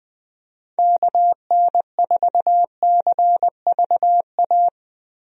Morse code, heard as a keyed tone, about 21 wpm, 700 Hz. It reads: KN4CVA
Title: USA Call Signs - 6 Characters - Repeated 2x - 20wpm